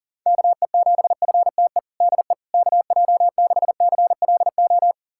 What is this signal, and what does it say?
Morse code, heard as a keyed tone, about 40 wpm, 700 Hz. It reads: KE7FTE DE KJ6CLO
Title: USA Callsign DE Callsign - 6 Characters - 40wpm